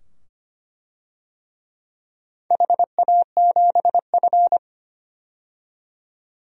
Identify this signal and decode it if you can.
Morse code, heard as a keyed tone, about 25 wpm, 700 Hz. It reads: HA7F
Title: DXCC-100 Call Signs - 4 Characters - 25wpm